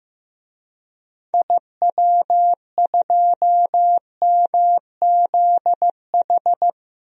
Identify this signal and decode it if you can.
Morse code, heard as a keyed tone, about 15 wpm, 700 Hz. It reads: IW2MZH
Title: DXCC-100 Call Signs - 6 Characters - Repeated 3x - 15wpm